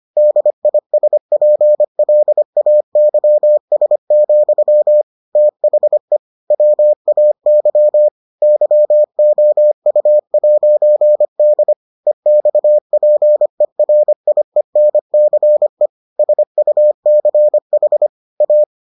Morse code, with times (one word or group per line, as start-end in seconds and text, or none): 0.2-5.0 DISPLAYS,
5.4-6.2 THE
6.5-8.1 WAY
8.4-11.7 YOU'D
12.1-15.9 EXPERIENCE
16.2-18.1 SUCH
18.4-18.6 A